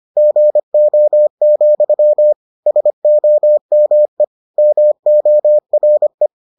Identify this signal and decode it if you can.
Morse code, heard as a keyed tone, about 25 wpm, 600 Hz. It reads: GO, SOME MORE